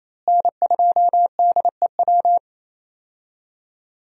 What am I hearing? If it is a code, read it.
Morse code, heard as a keyed tone, about 28 wpm, 700 Hz. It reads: N2DEW
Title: USA Call Signs - 5 Characters - Repeated 3x - 28wpm